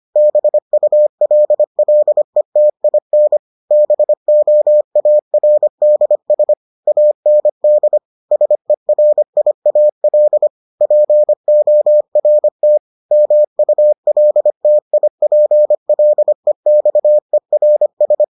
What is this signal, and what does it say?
Morse code, heard as a keyed tone, about 25 wpm, 600 Hz. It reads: BULLETIN BOARDS AND SERIAL PORT MULTIPLEXERS